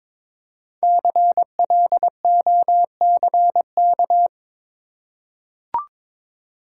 Morse code, heard as a keyed tone, about 22 wpm, 700 Hz. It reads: CLOCK
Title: Five Letter Words - 22wpm